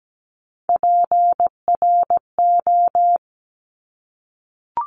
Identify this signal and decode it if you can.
Morse code, heard as a keyed tone, about 17 wpm, 700 Hz. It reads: PRO
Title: Top Three Letter Combinations in Words 17wpm